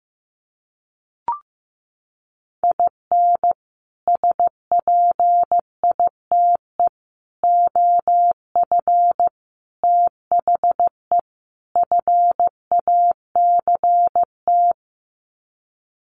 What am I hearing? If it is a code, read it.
Morse code, heard as a keyed tone, about 14 wpm, 700 Hz. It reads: IN SPITE OF THE FACT
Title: Sets of 5 Words Encore2 15wpm